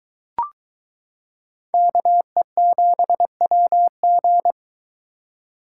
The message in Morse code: KE7WG

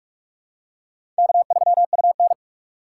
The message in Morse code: K3UN

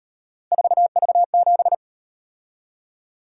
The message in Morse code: 4V7

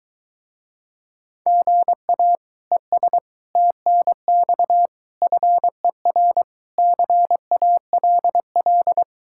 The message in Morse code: GA ES TNX FER CALL

